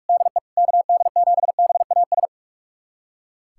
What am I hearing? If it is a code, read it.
Morse code, heard as a keyed tone, about 45 wpm, 700 Hz. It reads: DE KD7BAS
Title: USA Callsign DE Callsign - 6 Characters - 45wpm